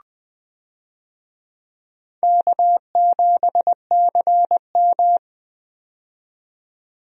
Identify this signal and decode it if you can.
Morse code, heard as a keyed tone, about 20 wpm, 700 Hz. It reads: K7CM
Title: USA Call Signs - 4 Characters - 20wpm